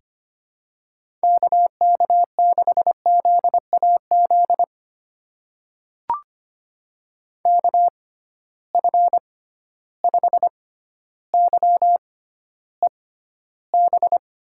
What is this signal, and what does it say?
Morse code, heard as a keyed tone, about 25 wpm, spaced as for 13 wpm, 700 Hz. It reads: KK6ZAZ KF5YEB
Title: USA Call Signs - 6 Characters - 6x Character Spacing - 25wpm